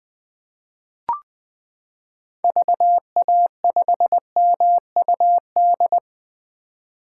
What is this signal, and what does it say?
Morse code, heard as a keyed tone, about 21 wpm, 700 Hz. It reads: VA5MUD
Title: 2023 POTA Activator Call Signs - 20wpm